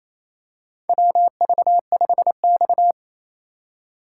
Morse code, heard as a keyed tone, about 28 wpm, 700 Hz. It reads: WV5X
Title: USA Call Signs - 4 Characters - 4x Character Spacing - 28wpm